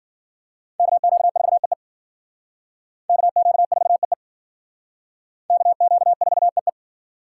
Morse code, heard as a keyed tone, about 50 wpm, 700 Hz. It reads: KQ4EE KQ4EE KQ4EE
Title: Speed Racing - USA 2x2 Call Signs 40wpm